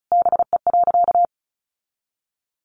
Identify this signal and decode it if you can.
Morse code, heard as a keyed tone, about 34 wpm, 700 Hz. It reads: BE.
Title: Sentences from Top 600 Words 35wpm